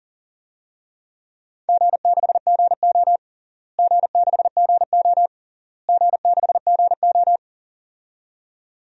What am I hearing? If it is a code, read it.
Morse code, heard as a keyed tone, about 40 wpm, 700 Hz. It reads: G6GO G6GO G6GO
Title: DXCC-100 Call Signs - 4 Characters - Repeated 3x - 40wpm